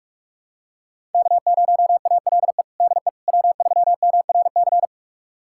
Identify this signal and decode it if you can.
Morse code, heard as a keyed tone, about 46 wpm, 700 Hz. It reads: K0ALE DE W3MRC
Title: USA Callsign DE Callsign - 5 Characters - 45wpm